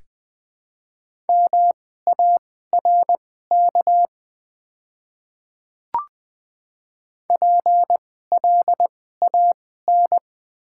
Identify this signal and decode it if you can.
Morse code, heard as a keyed tone, about 21 wpm, spaced as for 15 wpm, 700 Hz. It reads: MARK PLAN
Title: 2x Character Spacing - Top 500 Words - 20wpm